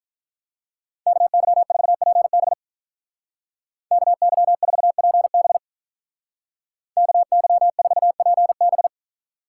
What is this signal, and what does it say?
Morse code, heard as a keyed tone, about 44 wpm, 700 Hz. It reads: KY4PB KY4PB KY4PB
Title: Speed Racing - USA 2x2 Call Signs 35wpm